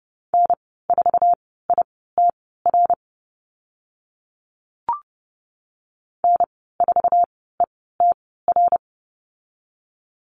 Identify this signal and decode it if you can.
Morse code, heard as a keyed tone, about 30 wpm, spaced as for 17 wpm, 700 Hz. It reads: N4ITR N4ETR
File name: Lesson 14g - Number 2 - Call Signs 30wpm